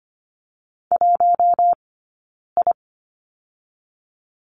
Morse code, heard as a keyed tone, about 25 wpm, spaced as for 9 wpm, 700 Hz. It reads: 1I
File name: Lesson 15d - Number 5 - Two Characters 25wpm